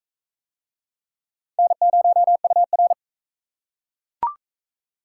Morse code, heard as a keyed tone, about 43 wpm, 700 Hz. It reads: N0UR